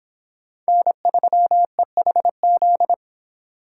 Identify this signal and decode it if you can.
Morse code, heard as a keyed tone, about 29 wpm, 700 Hz. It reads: N3EHZ